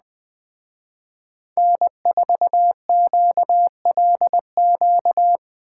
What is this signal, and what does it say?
Morse code, heard as a keyed tone, about 20 wpm, 700 Hz. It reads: N4QLQ
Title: Speed Racing - USA 1x3 Call Signs 17wpm